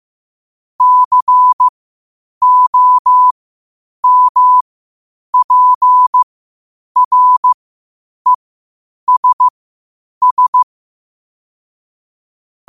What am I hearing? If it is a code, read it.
Morse code, heard as a keyed tone, about 15 wpm, spaced as for 8 wpm, 1000 Hz. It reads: COMPRESS
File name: qotdfiles / 8wpm